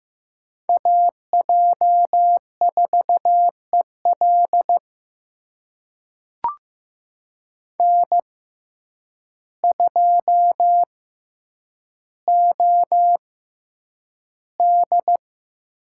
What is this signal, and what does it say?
Morse code, heard as a keyed tone, about 15 wpm, spaced as for 8 wpm, 700 Hz. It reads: AJ4EL N2OD
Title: USA Call Signs - 6x Character Spacing - 15wpm